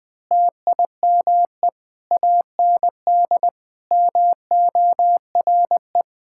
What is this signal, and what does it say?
Morse code, heard as a keyed tone, about 20 wpm, 700 Hz. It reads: TIME AND MORE
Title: Sentences with 2-Word Phrases - Spoken Beginning - 20wpm